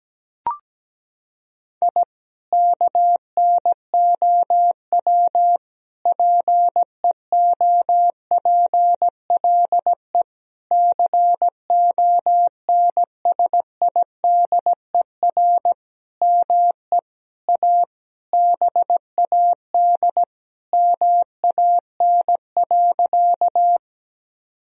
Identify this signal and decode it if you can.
Morse code, heard as a keyed tone, about 17 wpm, 700 Hz. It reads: I KNOW PEOPLE CONSIDER ME A BAD MAN.
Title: Sentences from Top 1000 Words - Review Part 4 - 17wpm